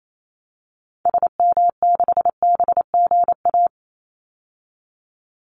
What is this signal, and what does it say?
Morse code, heard as a keyed tone, about 28 wpm, 700 Hz. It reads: SM6BGA
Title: Call Signs Repeated 2x - 28wpm